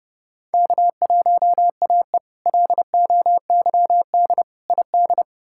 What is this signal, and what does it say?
Morse code, heard as a keyed tone, about 30 wpm, 700 Hz. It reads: K1AE LOYD ID
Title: K1USN SST QSO Practice - 30wpm